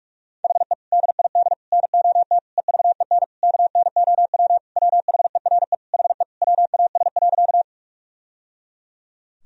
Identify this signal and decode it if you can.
Morse code, heard as a keyed tone, about 44 wpm, 700 Hz. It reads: HE DID NOT EVEN KNOW WHERE HE WAS.
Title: Sentences from Top 1000 Words - Review Part 1 - 45wpm